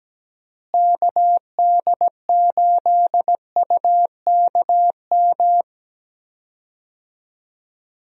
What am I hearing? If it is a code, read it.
Morse code, heard as a keyed tone, about 17 wpm, 700 Hz. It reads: KD8UKM
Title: USA Call Signs - 6 Characters - Repeated 2x - 17wpm